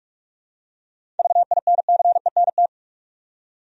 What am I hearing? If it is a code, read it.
Morse code, heard as a keyed tone, about 42 wpm, 700 Hz. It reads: VINCENT
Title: USA Common Names - 45wpm